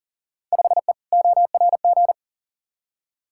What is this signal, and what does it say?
Morse code, heard as a keyed tone, about 40 wpm, 700 Hz. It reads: 5E ORG
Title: Field Day Exchanges Encore - 40wpm